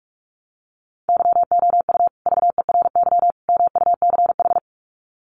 Morse code, heard as a keyed tone, about 44 wpm, 700 Hz. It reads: YOU VERY MUCH
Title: Sets of 3 Words 45wpm